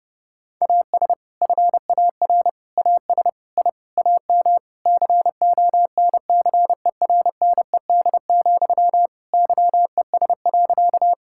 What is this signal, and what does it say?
Morse code, heard as a keyed tone, about 30 wpm, 700 Hz. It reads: AS FAR AS I AM CONCERNED, YES.
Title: Sentences from Top 1000 Words - Review Part 6 - 30wpm